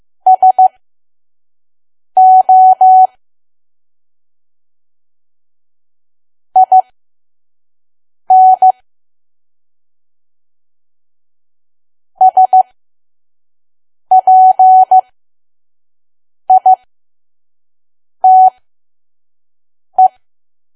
Morse code, heard as a keyed tone, about 15 wpm, spaced as for 5 wpm, 750 Hz. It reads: SO IN SPITE
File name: Archive / 5 WPM files / PM